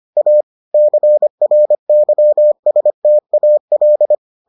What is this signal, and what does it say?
Morse code, heard as a keyed tone, about 25 wpm, 600 Hz. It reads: A CRYSTAL